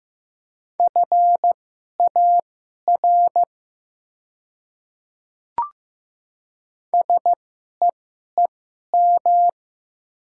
Morse code, heard as a keyed tone, about 15 wpm, spaced as for 11 wpm, 690 Hz. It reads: FAR SEEM
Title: 2x Character Spacing - Top 300 Words - 15wpm